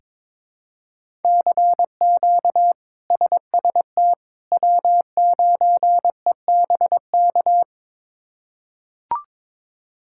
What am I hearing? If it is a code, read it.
Morse code, heard as a keyed tone, about 22 wpm, 700 Hz. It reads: CQ SST W9EBK
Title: K1USN SST QSO Practice - 22wpm